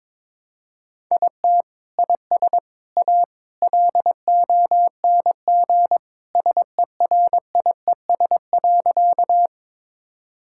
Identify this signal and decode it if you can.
Morse code, heard as a keyed tone, about 22 wpm, 700 Hz. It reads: IT IS A LONG SERIES.
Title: Sentences from Top 1000 Words - Review Part 5 - 22wpm